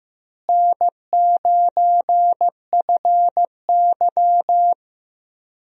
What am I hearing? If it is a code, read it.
Morse code, heard as a keyed tone, about 16 wpm, 700 Hz. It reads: N9FY
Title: USA Call Signs - 4 Characters - Repeated 2x - 15wpm